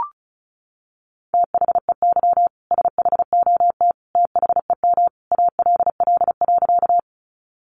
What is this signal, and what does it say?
Morse code, heard as a keyed tone, about 34 wpm, 700 Hz. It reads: THEY SHOT THEM ALL.
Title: Sentences from Top 1100 Words 35wpm